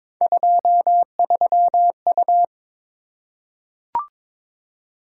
Morse code, heard as a keyed tone, about 22 wpm, 700 Hz. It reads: 23U
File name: practice / 22wpm / Three Letter-Number Combinations - 22wpm